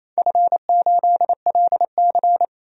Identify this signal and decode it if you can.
Morse code, heard as a keyed tone, about 28 wpm, 700 Hz. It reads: F8LC
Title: DXCC-100 Call Signs - 4 Characters - 28wpm